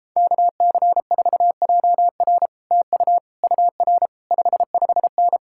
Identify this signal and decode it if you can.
Morse code, heard as a keyed tone, about 37 wpm, 700 Hz. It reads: KC4JR TU UR 55N